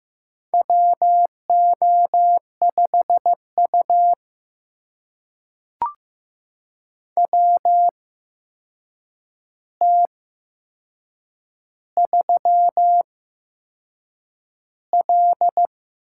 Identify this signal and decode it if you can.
Morse code, heard as a keyed tone, about 15 wpm, spaced as for 6 wpm, 700 Hz. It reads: WO5U WT3L
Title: USA Call Signs - 4 Characters - 8x Character Spacing - 15wpm